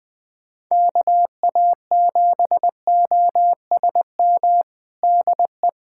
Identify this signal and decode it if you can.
Morse code, heard as a keyed tone, about 21 wpm, 700 Hz. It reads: KA7OSM DE
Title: USA Callsign DE Callsign 20wpm